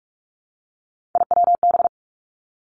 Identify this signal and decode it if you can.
Morse code, heard as a keyed tone, about 46 wpm, 700 Hz. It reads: IWB
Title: Three Challenging Letter-Number Combinations 45wpm